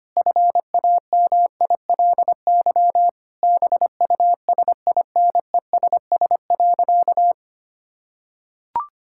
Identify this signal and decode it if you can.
Morse code, heard as a keyed tone, about 25 wpm, 700 Hz. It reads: FAMILY BUSINESS.
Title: Sentences from Top 1000 Words - Review Part 6 - 25wpm